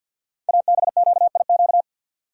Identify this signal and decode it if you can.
Morse code, heard as a keyed tone, about 50 wpm, 690 Hz. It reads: ABQIQ
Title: Code Groups - 5 Letters - 50wpm